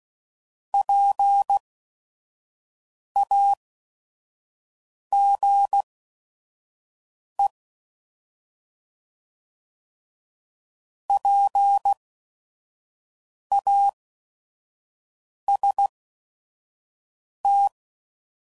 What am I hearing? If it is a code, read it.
Morse code, heard as a keyed tone, about 16 wpm, spaced as for 5 wpm, 800 Hz. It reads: PAGE PAST
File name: Prefixes / By First Letter / Ps